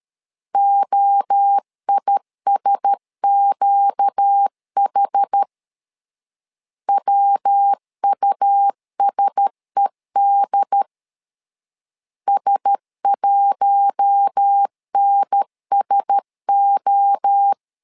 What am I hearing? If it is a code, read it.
Morse code, heard as a keyed tone, about 13 wpm, 800 Hz. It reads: OISQH WUSED S1NSO
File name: LEZIONI / 4 Q1S 12wpm 1802